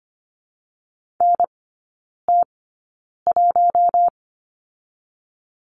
Morse code, heard as a keyed tone, about 25 wpm, spaced as for 9 wpm, 700 Hz. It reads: NT1
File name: Lesson 17e - Letter u - Three Characters 25wpm